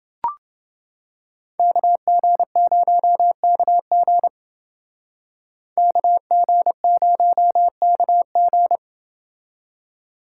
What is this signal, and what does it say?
Morse code, heard as a keyed tone, about 28 wpm, 700 Hz. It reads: KG0KG KG0KG